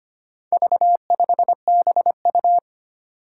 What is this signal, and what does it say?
Morse code, heard as a keyed tone, about 25 wpm, 700 Hz. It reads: V5BU
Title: Four Challenging Letter-Number Combinations 25wpm